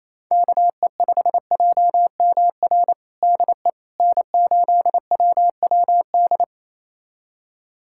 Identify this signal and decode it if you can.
Morse code, heard as a keyed tone, about 28 wpm, 700 Hz. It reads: KE5JMR DE N8WWD
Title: USA Callsign DE Callsign 28wpm